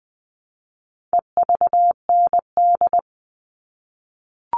Morse code, heard as a keyed tone, about 20 wpm, 700 Hz. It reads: EVND
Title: All Cut Numbers - 4-Digit 20wpm